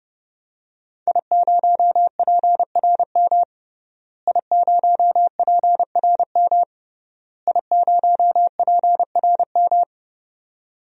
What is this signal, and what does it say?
Morse code, heard as a keyed tone, about 30 wpm, 700 Hz. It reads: I0PRM I0PRM I0PRM
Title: DXCC-100 Call Signs - 5 Characters - Repeated 3x - 30wpm